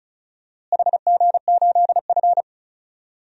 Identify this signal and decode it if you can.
Morse code, heard as a keyed tone, about 36 wpm, 700 Hz. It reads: HG8F
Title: DXCC-100 Call Signs - 4 Characters - Repeated 2x - 35wpm